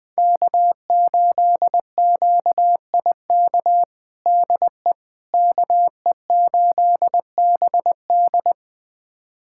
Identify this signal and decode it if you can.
Morse code, heard as a keyed tone, about 21 wpm, 700 Hz. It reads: K8QIK DE KE8BD
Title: USA Callsign DE Callsign - 5 Characters - 20wpm